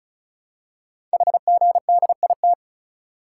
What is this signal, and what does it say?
Morse code, heard as a keyed tone, about 36 wpm, 700 Hz. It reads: HGDIT
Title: Code Groups - 5 Letters - 35wpm